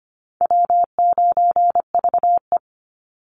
Morse code, heard as a keyed tone, about 25 wpm, 700 Hz. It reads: W9VE
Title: Easy Call Signs - 25wpm